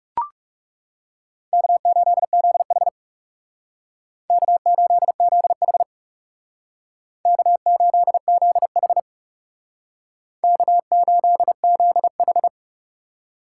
Morse code, heard as a keyed tone, about 38 wpm, 700 Hz. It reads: K8ZH K8ZH K8ZH K8ZH